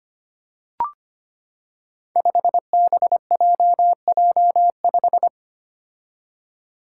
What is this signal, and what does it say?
Morse code, heard as a keyed tone, about 25 wpm, 700 Hz. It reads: 5BJJ5
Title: Five Challenging Letter-Number Combinations 25wpm